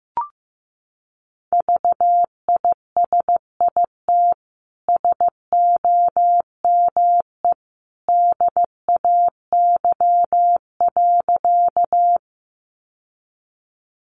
Morse code, heard as a keyed tone, about 15 wpm, 700 Hz. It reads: VISIT SOME DAY.
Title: Sentences from Top 900 Words 15wpm